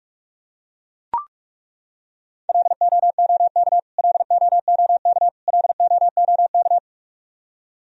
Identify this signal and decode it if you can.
Morse code, heard as a keyed tone, about 42 wpm, 700 Hz. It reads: LOOK LOOK LOOK
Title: Top 100 Words Repeated 45wpm